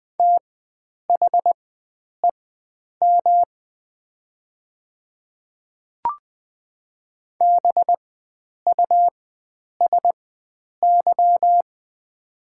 Morse code, heard as a keyed tone, about 20 wpm, spaced as for 9 wpm, 700 Hz. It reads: THEM BUSY
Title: 4x Character Spacing - Top 500 Words - 20wpm